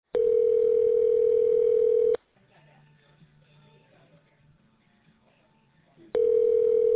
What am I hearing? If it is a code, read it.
Ringback tone (North America), 440 + 480 Hz.